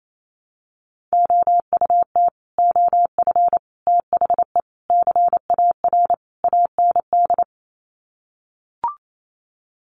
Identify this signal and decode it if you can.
Morse code, heard as a keyed tone, about 28 wpm, 690 Hz. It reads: OUT OF THE CAR AND
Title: Sets of 5 Words Encore 28wpm